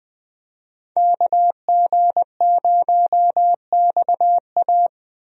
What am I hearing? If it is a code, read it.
Morse code, heard as a keyed tone, about 20 wpm, 700 Hz. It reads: KG0XA
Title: USA Call Signs - 5 Characters - 20wpm